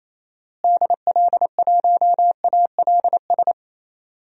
Morse code, heard as a keyed tone, about 28 wpm, 700 Hz. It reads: DL1ALS
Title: DXCC-100 Call Signs - 6 Characters - Repeated 3x - 28wpm